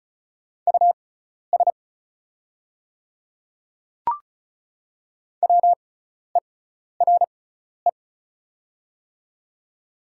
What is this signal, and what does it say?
Morse code, heard as a keyed tone, about 36 wpm, spaced as for 12 wpm, 690 Hz. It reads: US WERE
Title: 6x Character Spacing - Top 300 Words - 35wpm